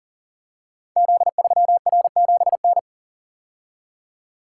Morse code, heard as a keyed tone, about 42 wpm, 700 Hz. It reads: Z3R7N
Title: Code Groups - 5 Letter:Numbers - 40wpm